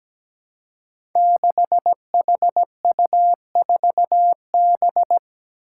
Morse code, heard as a keyed tone, about 17 wpm, 700 Hz. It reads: 6HU4B
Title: Five Challenging Letter-Number Combinations 17wpm